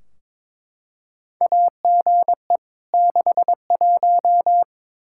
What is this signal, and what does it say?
Morse code, heard as a keyed tone, about 22 wpm, 700 Hz. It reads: AGE 61
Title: K1USN SST QSO Practice 22wpm